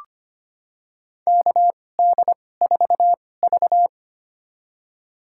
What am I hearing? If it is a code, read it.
Morse code, heard as a keyed tone, about 25 wpm, spaced as for 18 wpm, 700 Hz. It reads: KD4V
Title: USA Call Signs - 4 Characters - 2x Character Spacing - 25wpm